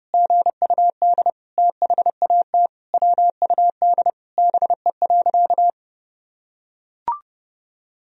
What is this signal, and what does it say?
Morse code, heard as a keyed tone, about 30 wpm, 700 Hz. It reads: GUD THAT WUD BE.
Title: Sentences from Top 200 Words and Abbreviations 30wpm